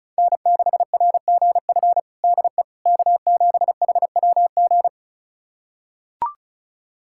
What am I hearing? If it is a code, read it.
Morse code, heard as a keyed tone, about 36 wpm, 700 Hz. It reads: N6RGF DE K7HWG
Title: USA Callsign DE Callsign - 5 Characters - 35wpm